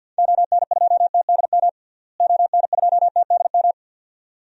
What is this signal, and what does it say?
Morse code, heard as a keyed tone, about 46 wpm, 700 Hz. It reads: ON2TDM ON2TDM
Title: DXCC-100 Call Signs - 6 Characters - Repeated 2x - 50wpm